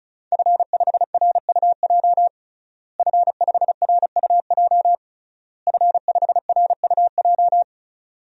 Morse code, heard as a keyed tone, about 36 wpm, 700 Hz. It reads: F5RUJ F5RUJ F5RUJ
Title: DXCC-100 Call Signs - 5 Characters - Repeated 3x - 35wpm